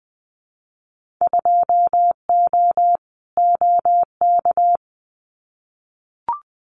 Morse code, heard as a keyed tone, about 20 wpm, 700 Hz. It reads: 2O OK
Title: Winter Field Day Exchanges 20wpm